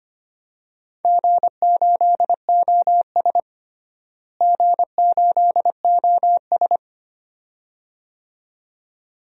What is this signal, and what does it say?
Morse code, heard as a keyed tone, about 25 wpm, 700 Hz. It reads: G8OS G8OS
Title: DXCC-100 Call Signs - 4 Characters - Repeated 2x - 25wpm